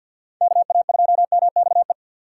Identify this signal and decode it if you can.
Morse code, heard as a keyed tone, about 52 wpm, 700 Hz. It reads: KA2MXE
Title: USA Call Signs - 6 Characters - 4x Character Spacing - 50wpm